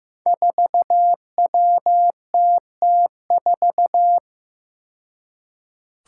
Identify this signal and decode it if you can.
Morse code, heard as a keyed tone, about 15 wpm, 700 Hz. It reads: 4WTT4
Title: Five Challenging Letter-Number Combinations 15wpm